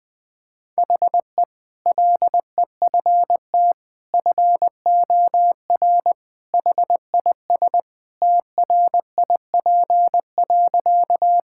Morse code, heard as a keyed tone, about 20 wpm, 700 Hz. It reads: HE LEFT FOR HIS TRIP.
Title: Sentences with 2-Word Phrases - Spoken Beginning - 20wpm